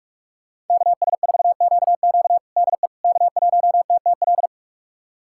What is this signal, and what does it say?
Morse code, heard as a keyed tone, about 42 wpm, 700 Hz. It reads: KS4QQ DE K1TTL
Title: USA Callsign DE Callsign - 5 Characters - 45wpm